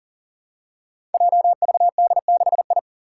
Morse code, heard as a keyed tone, about 42 wpm, 700 Hz. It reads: JVD6I
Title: Five Challenging Letter-Number Combinations 40wpm